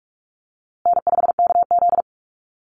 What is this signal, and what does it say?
Morse code, heard as a keyed tone, about 44 wpm, 700 Hz. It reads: N5KZ
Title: USA Call Signs Repeated 3x 45wpm